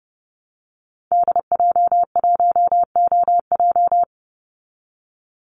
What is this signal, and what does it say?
Morse code, heard as a keyed tone, about 30 wpm, 700 Hz. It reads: DJ1OJ
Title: Call Signs Repeated 2x - 30wpm